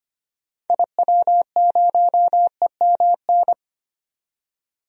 Morse code, heard as a keyed tone, about 25 wpm, 700 Hz. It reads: IW0EMN
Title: DXCC-100 Call Signs - 6 Characters - 25wpm